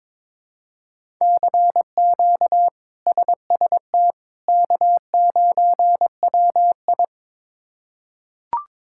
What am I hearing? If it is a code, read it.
Morse code, heard as a keyed tone, about 22 wpm, 700 Hz. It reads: CQ SST K9WI
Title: K1USN SST QSO Practice - 22wpm